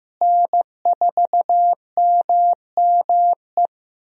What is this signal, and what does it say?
Morse code, heard as a keyed tone, about 15 wpm, 700 Hz. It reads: N4MME
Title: USA Call Signs - 5 Characters - Repeated 3x - 15wpm